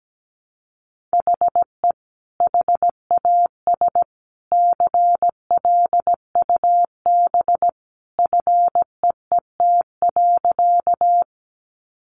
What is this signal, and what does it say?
Morse code, heard as a keyed tone, about 17 wpm, 700 Hz. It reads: HE HAS CLUB FEET.
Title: Sentences from Top 1200 Words 17wpm